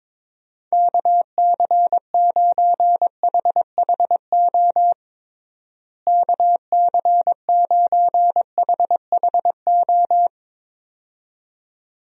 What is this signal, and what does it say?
Morse code, heard as a keyed tone, about 22 wpm, 700 Hz. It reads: KC9HHO KC9HHO
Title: USA Call Signs - 6 Characters - Repeated 2x - 22wpm